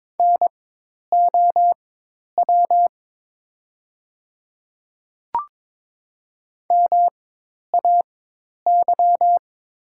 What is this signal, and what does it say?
Morse code, heard as a keyed tone, about 22 wpm, spaced as for 10 wpm, 700 Hz. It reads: NOW MAY
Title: 4x Character Spacing - Top 200 Words - 22wpm